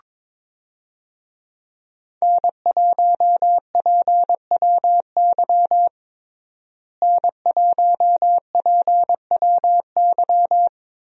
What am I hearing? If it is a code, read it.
Morse code, heard as a keyed tone, about 22 wpm, 700 Hz. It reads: N1PWY N1PWY
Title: USA Call Signs - 5 Characters - Repeated 2x - 22wpm